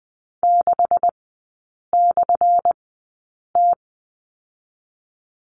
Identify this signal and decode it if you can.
Morse code, heard as a keyed tone, about 20 wpm, spaced as for 8 wpm, 700 Hz. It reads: 6/T
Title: 178 Lesson 29e - Slash - Three Characters 20wpm